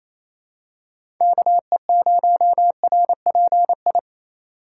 Morse code, heard as a keyed tone, about 27 wpm, 700 Hz. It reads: KE0RPI
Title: USA Call Signs - 4x Character Spacing - 28wpm